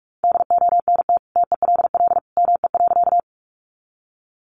Morse code, heard as a keyed tone, about 44 wpm, 700 Hz. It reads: DONT TELL ME.
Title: Sentences from Top 300 Words 45wpm